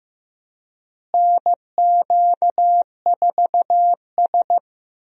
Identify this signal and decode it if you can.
Morse code, heard as a keyed tone, about 15 wpm, 700 Hz. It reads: NQ4S
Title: USA Call Signs - 4 Characters - Repeated 2x - 15wpm